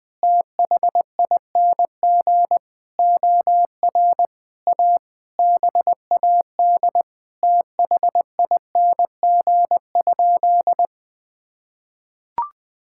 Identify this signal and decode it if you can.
Morse code, heard as a keyed tone, about 20 wpm, 700 Hz. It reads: THING OR A BAD THING?
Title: Sentences with 7-Word Phrases - 20wpm